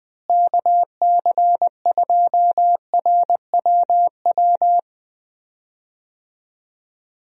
Morse code, heard as a keyed tone, about 20 wpm, 700 Hz. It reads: KC2RWW
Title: USA Call Signs - 6 Characters - Repeated 2x - 20wpm